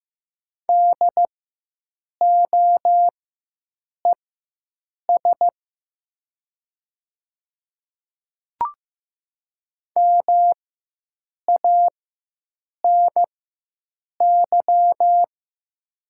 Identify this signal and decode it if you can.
Morse code, heard as a keyed tone, about 15 wpm, spaced as for 7 wpm, 690 Hz. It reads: DOES MANY
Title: 4x Character Spacing - Top 200 Words - 15wpm